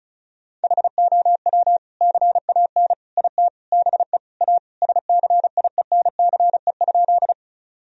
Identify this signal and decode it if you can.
Morse code, heard as a keyed tone, about 34 wpm, 700 Hz. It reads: HOW CAN IT BE A SCIENCE?
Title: Sentences from Top 1000 Words - Review Part 2 - 35wpm